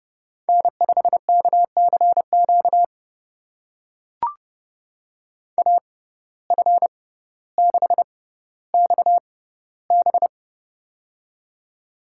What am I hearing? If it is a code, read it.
Morse code, heard as a keyed tone, about 31 wpm, spaced as for 16 wpm, 700 Hz. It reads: N5KCQ AF6XB
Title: USA Call Signs - 5 Characters - 6x Character Spacing - 30wpm